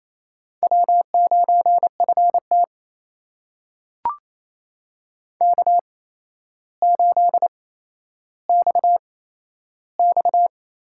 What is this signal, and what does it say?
Morse code, heard as a keyed tone, about 28 wpm, spaced as for 12 wpm, 700 Hz. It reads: W9FT K8XX
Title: USA Call Signs - 4 Characters - 8x Character Spacing - 28wpm